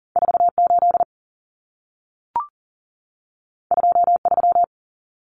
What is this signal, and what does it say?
Morse code, heard as a keyed tone, about 40 wpm, 700 Hz. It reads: 48 23
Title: Common Cut Numbers - 2-Digit 40wpm